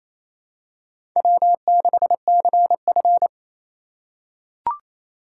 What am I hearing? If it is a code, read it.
Morse code, heard as a keyed tone, about 28 wpm, 700 Hz. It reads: W6CF
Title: USA Call Signs - 4 Characters - 8x Character Spacing - 28wpm